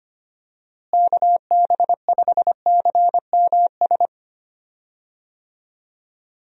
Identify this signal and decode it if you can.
Morse code, heard as a keyed tone, about 25 wpm, 700 Hz. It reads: KB5CMS
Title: USA Call Signs - 6 Characters - Repeated 3x - 25wpm